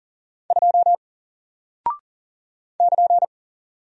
Morse code, heard as a keyed tone, about 40 wpm, 700 Hz. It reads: 2 <KN>
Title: Single Character (Letter:Number:Punctuation:Prosign) Alt 40wpm